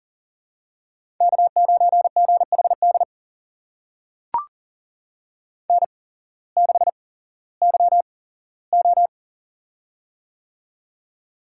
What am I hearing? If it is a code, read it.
Morse code, heard as a keyed tone, about 40 wpm, spaced as for 19 wpm, 690 Hz. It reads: K9GHD N6YO
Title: USA Call Signs - 8x Character Spacing - 40wpm